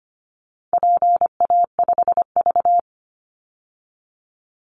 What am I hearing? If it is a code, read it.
Morse code, heard as a keyed tone, about 25 wpm, 700 Hz. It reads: PA5V
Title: Call Signs Repeated 3x - 25wpm